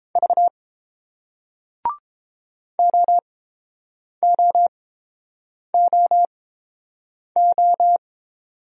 Morse code, heard as a keyed tone, about 30 wpm, spaced as for 7 wpm, 700 Hz. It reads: V OOOO